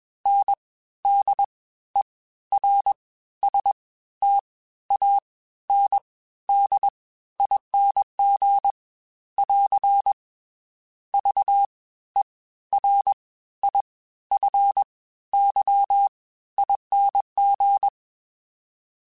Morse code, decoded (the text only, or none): NDERSTANDING <AR> VERIFYING